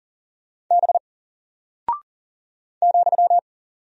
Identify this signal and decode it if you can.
Morse code, heard as a keyed tone, about 40 wpm, 700 Hz. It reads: B ,
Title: Single Character (Letter:Number:Punctuation:Prosign) 40wpm